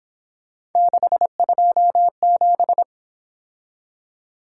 Morse code, heard as a keyed tone, about 26 wpm, 700 Hz. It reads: 627
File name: Speed-Racing - Three Letter-Number Combinations - 22wpm